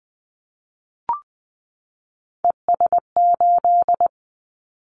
Morse code, heard as a keyed tone, about 20 wpm, 700 Hz. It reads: ES8
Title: Callsign Prefixes for Top 100 DXCC - 20wpm